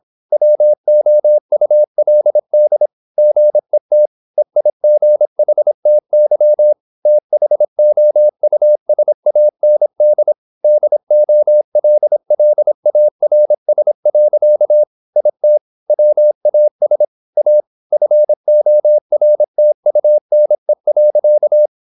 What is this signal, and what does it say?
Morse code, heard as a keyed tone, about 26 wpm, 600 Hz. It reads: WOULD GET EIGHTY THOUSAND DOLLARS. IT WAS A FORTUNE.
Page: Never 26wpm | CW med Gnister